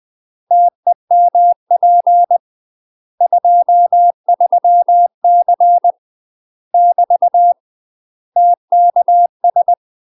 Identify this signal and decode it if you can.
Morse code, heard as a keyed tone, about 20 wpm, 700 Hz. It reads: TEMP 23C <BT> TKS
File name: UK MASTER QSOs TRAINING/20WPM